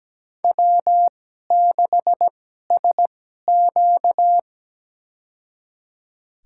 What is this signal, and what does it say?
Morse code, heard as a keyed tone, about 17 wpm, spaced as for 12 wpm, 700 Hz. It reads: W6SQ
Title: USA Call Signs - 4 Characters - 2x Character Spacing - 17wpm